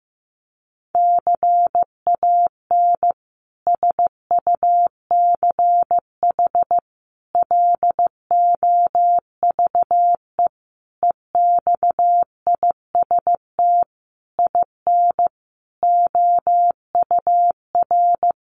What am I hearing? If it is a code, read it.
Morse code, heard as a keyed tone, about 15 wpm, 700 Hz. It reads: CAN SUCH LOVE EXIST IN OUR
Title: Sentences from Top 1100 Words 15wpm